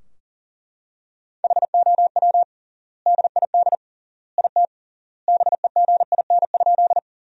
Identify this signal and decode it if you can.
Morse code, heard as a keyed tone, about 40 wpm, 700 Hz. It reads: HOW DID IT BEGIN?
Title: 3x Word-Spacing - Sentences from Top 300 Words - 40wpm